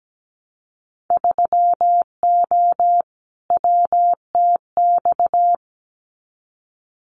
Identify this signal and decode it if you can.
Morse code, heard as a keyed tone, about 17 wpm, 700 Hz. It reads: 3O WTX
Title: Winter Field Day Exchanges Encore 17wpm